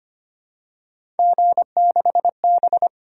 Morse code, heard as a keyed tone, about 25 wpm, 700 Hz. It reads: G6B
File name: Three Letter-Number Combinations - 25wpm